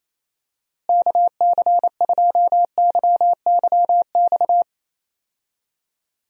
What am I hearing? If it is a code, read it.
Morse code, heard as a keyed tone, about 28 wpm, 700 Hz. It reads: KC2YYX
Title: USA Call Signs - 6 Characters - Repeated 2x - 28wpm